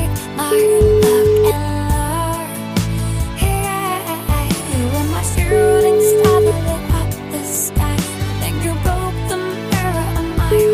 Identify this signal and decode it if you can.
Ringback tone (Europe, 425 Hz), with noise.